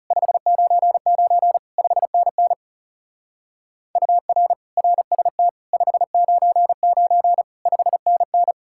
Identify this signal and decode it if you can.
Morse code, heard as a keyed tone, about 36 wpm, 700 Hz. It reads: 599 5NN UR RST 599 5NN
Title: Speed-Racing Common QSO Phrases - 30wpm